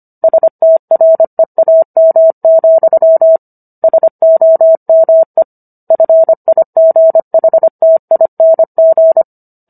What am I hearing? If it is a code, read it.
Morse code, heard as a keyed tone, about 25 wpm, 650 Hz. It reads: STREAM, SOME FIGHTING